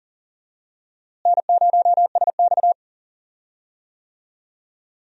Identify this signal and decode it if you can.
Morse code, heard as a keyed tone, about 40 wpm, 700 Hz. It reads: N0SX
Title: USA Call Signs - 4 Characters - Repeated 3x - 40wpm